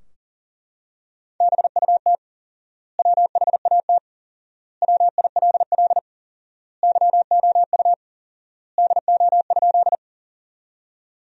Morse code, heard as a keyed tone, about 40 wpm, 700 Hz. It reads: BUT WHAT WILL YOU DO?
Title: 4x Word-Spacing - Sentences from Top 100 Words - 40wpm